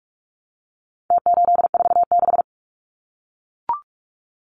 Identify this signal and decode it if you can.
Morse code, heard as a keyed tone, about 44 wpm, 700 Hz. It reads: T846
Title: Numbers - Contest Style - 4-Digit 45wpm